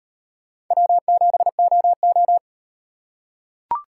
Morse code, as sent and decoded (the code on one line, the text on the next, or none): .-- --... --- ---
W7OO